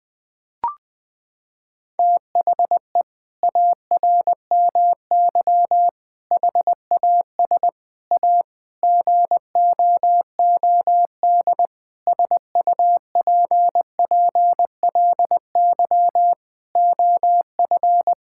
Morse code, heard as a keyed tone, about 20 wpm, 700 Hz. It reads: THE ARMY HAS A GOOD SUPPLY OF
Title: Sentences from Top 1000 Words - Review Part 5 - 20wpm